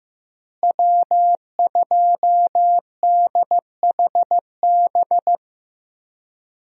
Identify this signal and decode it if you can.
Morse code, heard as a keyed tone, about 15 wpm, 700 Hz. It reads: W2DHB
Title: USA Call Signs - 5 Characters - Repeated 3x - 15wpm